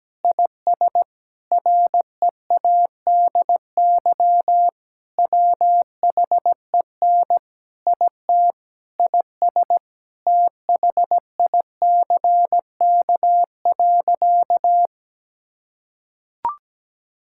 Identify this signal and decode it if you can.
Morse code, heard as a keyed tone, about 17 wpm, 700 Hz. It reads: IS READY WHEN IT IS THICK.
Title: Sentences from Top 1600 Words 17wpm